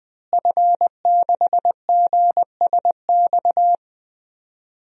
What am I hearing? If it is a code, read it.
Morse code, heard as a keyed tone, about 21 wpm, 700 Hz. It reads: F6GSX
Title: DXCC-100 Call Signs - 5 Characters - Repeated 2x - 20wpm